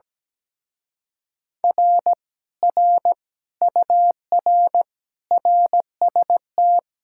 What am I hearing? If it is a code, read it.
Morse code, heard as a keyed tone, about 17 wpm, 700 Hz. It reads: R R UR RST
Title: Speed-Racing Common QSO Phrases Encore - 17wpm